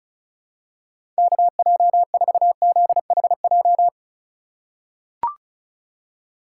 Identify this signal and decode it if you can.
Morse code, heard as a keyed tone, about 36 wpm, 700 Hz. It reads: KJ4ZHJ
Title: USA Call Signs - 6 Characters - 4x Character Spacing - 35wpm